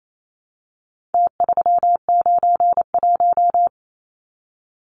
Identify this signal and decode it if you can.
Morse code, heard as a keyed tone, about 28 wpm, 700 Hz. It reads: T391
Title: Numbers - Contest Style - 4-Digit 28wpm